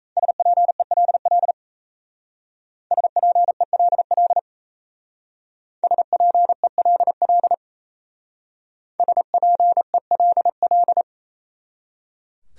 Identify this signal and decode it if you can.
Morse code, heard as a keyed tone, about 35 wpm, 690 Hz. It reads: SPELL SPELL SPELL SPELL